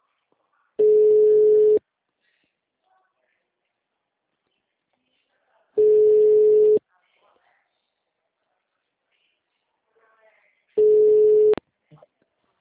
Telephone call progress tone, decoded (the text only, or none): ringback tone